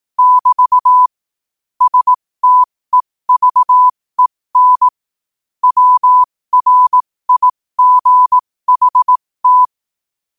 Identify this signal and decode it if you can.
Morse code, heard as a keyed tone, about 18 wpm, spaced as for 15 wpm, 1000 Hz. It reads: <BT> STEVEN WRIGHT
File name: qotdfiles / 15wpm